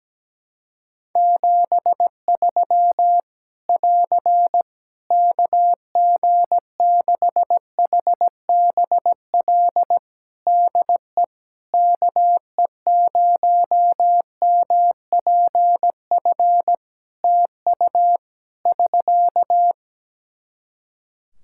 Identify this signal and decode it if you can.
Morse code, heard as a keyed tone, about 17 wpm, 700 Hz. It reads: 73 <AR> KG6HBL DE KE0MPF TU <SK>
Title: Practice QSO - Exchange 4 - Ending - 17wpm